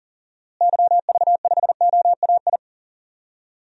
Morse code, heard as a keyed tone, about 37 wpm, 700 Hz. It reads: YV5OAI
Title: DXCC-100 Call Signs - 6 Characters - Repeated 3x - 40wpm